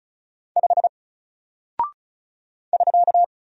Morse code, heard as a keyed tone, about 36 wpm, 700 Hz. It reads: 5 <SK>
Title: Single Character (Letter:Number:Punctuation:Prosign) Alt 35wpm